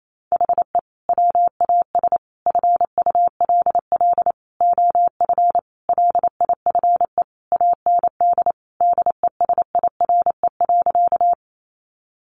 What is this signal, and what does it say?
Morse code, heard as a keyed tone, about 28 wpm, 700 Hz. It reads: HE WAS FULL OF LIFE AND DESIRE.
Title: Sentences from Top 1100 Words 28wpm